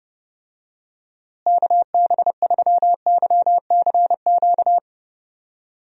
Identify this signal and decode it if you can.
Morse code, heard as a keyed tone, about 31 wpm, 700 Hz. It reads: KB3YCQ
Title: USA Call Signs - 6 Characters - Repeated 3x - 30wpm